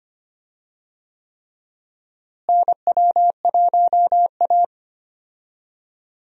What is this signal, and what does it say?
Morse code, heard as a keyed tone, about 25 wpm, 700 Hz. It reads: NW1A
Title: USA Call Signs - 4 Characters - Repeated 3x - 25wpm